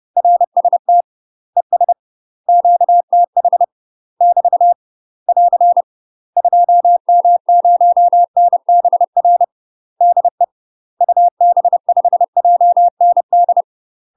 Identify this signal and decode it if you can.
Morse code, heard as a keyed tone, about 30 wpm, 700 Hz. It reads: RST ES QTH <BT> <AR> 2M0NBR DE UB5JND
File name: UK MASTER QSOs TRAINING / 30WPM / UK MASTER QSOs 0009